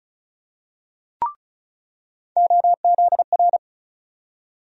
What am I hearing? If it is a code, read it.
Morse code, heard as a keyed tone, about 36 wpm, 700 Hz. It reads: OZR
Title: USA Call Sign Postfixes - 3 Letters 35wpm